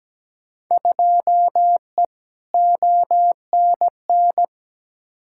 Morse code, heard as a keyed tone, about 17 wpm, 700 Hz. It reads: 2E ONN
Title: Field Day Exchanges Encore - 17wpm